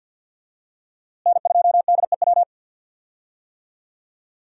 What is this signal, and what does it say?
Morse code, heard as a keyed tone, about 51 wpm, 700 Hz. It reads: N2DEW
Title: USA Call Signs - 5 Characters - Repeated 3x - 50wpm